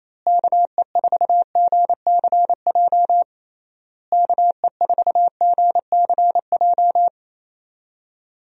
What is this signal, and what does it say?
Morse code, heard as a keyed tone, about 27 wpm, 700 Hz. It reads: KE4GCJ KE4GCJ
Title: USA Call Signs - 6 Characters - Repeated 3x - 28wpm